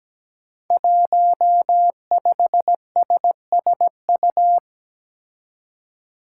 Morse code, heard as a keyed tone, about 17 wpm, 700 Hz. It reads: 15SSU
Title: Five Challenging Letter-Number Combinations 17wpm